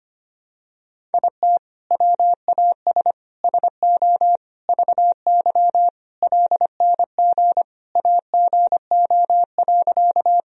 Morse code, heard as a keyed tone, about 25 wpm, 700 Hz. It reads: IT WAS SO VY LNG AGO.
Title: Sentences from Top 500 Words and Abbreviations 25wpm